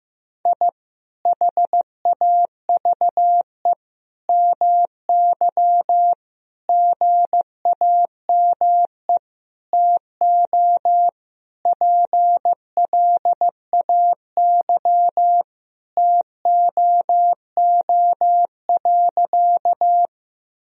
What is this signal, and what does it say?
Morse code, heard as a keyed tone, about 15 wpm, 700 Hz. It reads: I HAVE MY GAME TO PLAY TOO.
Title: Sentences from Top 1000 Words - Review Part 3 - 15wpm